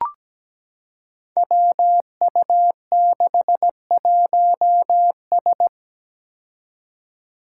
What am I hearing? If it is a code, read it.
Morse code, heard as a keyed tone, about 17 wpm, 700 Hz. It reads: WU61S
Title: Five Challenging Letter-Number Combinations 17wpm